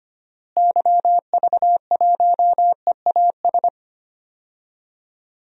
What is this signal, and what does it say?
Morse code, heard as a keyed tone, about 25 wpm, 700 Hz. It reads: YV1EAS
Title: DXCC-100 Call Signs - 6 Characters - Repeated 2x - 25wpm